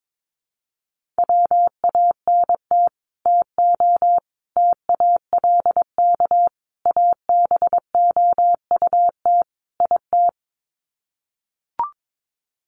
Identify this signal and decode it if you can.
Morse code, heard as a keyed tone, about 22 wpm, 700 Hz. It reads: WANT TO TALK ABOUT IT
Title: Sets of 5 Words Encore 22wpm